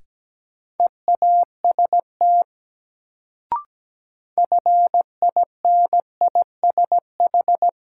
Morse code, heard as a keyed tone, about 17 wpm, 700 Hz. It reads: EAST FINISH
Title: Sentences from Top 1000 Words - Review Part 3 - New Word Drill - 17wpm